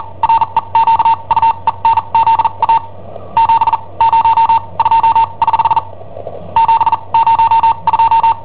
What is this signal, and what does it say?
Morse code, heard as a keyed tone, about 43 wpm, spaced as for 31 wpm, 910 Hz. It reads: REQUENZA 702 <HH> 702
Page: Cassetta esercizi preparata da I7OHP per IW5ELP